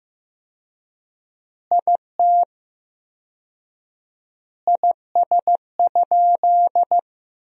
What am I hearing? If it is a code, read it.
Morse code, heard as a keyed tone, about 15 wpm, 700 Hz. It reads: IT IS?
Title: 4x Word-Spacing - Sentences from Top 100 Words - 15wpm